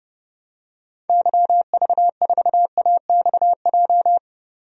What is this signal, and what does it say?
Morse code, heard as a keyed tone, about 30 wpm, 700 Hz. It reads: YV4AXJ
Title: DXCC-100 Call Signs - 6 Characters - 30wpm